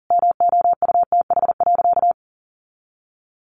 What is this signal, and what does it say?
Morse code, heard as a keyed tone, about 40 wpm, 700 Hz. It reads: MOUTH.
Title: Sentences from Top 1500 Words 40wpm